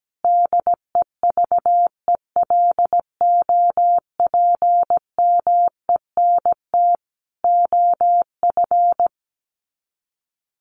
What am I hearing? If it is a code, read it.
Morse code, heard as a keyed tone, about 17 wpm, 700 Hz. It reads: DEVELOPMENT OF
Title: Sets of 3 Words 17wpm